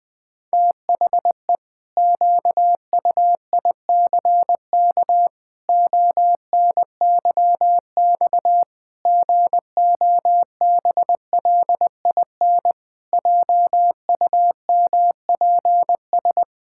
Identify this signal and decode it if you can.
Morse code, heard as a keyed tone, about 20 wpm, 700 Hz. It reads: THE QUICK ONYX GOBLIN JUMPS
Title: Pangram Challenge v2 - 20wpm